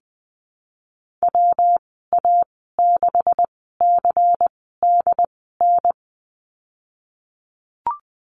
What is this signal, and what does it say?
Morse code, heard as a keyed tone, about 20 wpm, spaced as for 14 wpm, 700 Hz. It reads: WA6CDN
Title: 140 Lesson 23g - Letter f - Call Signs 20wpm